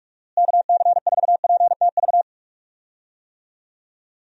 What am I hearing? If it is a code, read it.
Morse code, heard as a keyed tone, about 47 wpm, 700 Hz. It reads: KC4PTV
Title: USA Call Signs - 6 Characters - Repeated 2x - 45wpm